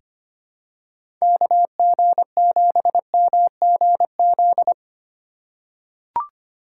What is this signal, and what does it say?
Morse code, heard as a keyed tone, about 25 wpm, 700 Hz. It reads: KG7MGZ
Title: USA Call Signs - 6 Characters - 4x Character Spacing - 25wpm